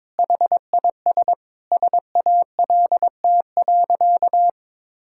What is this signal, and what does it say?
Morse code, heard as a keyed tone, about 21 wpm, 700 Hz. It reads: HIS SALT.
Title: Sentences from Top 1000 Words - Review Part 5 - 22wpm